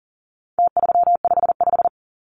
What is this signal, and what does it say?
Morse code, heard as a keyed tone, about 40 wpm, 700 Hz. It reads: T355
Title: Numbers - Contest Style - 4-Digit 40wpm